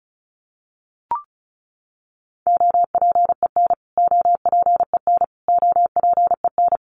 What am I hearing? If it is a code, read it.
Morse code, heard as a keyed tone, about 34 wpm, 700 Hz. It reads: OPEN OPEN OPEN
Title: Top 500 Words Repeated 35wpm